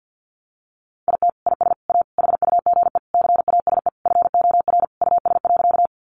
Morse code, heard as a keyed tone, about 50 wpm, 700 Hz. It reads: IT IS A HUGE CASE FOR US.
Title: Sentences from Top 1100 Words 50wpm